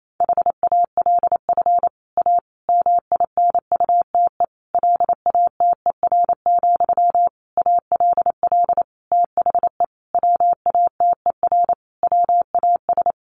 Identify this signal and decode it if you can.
Morse code, heard as a keyed tone, about 28 wpm, 700 Hz. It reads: HALF A MINUTE LATER, ALL THE WATER WAS
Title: Sentences from Top 1100 Words 28wpm